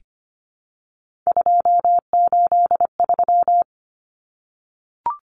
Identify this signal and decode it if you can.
Morse code, heard as a keyed tone, about 25 wpm, 700 Hz. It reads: 283
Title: Common Cut Numbers - 3-Digit 25wpm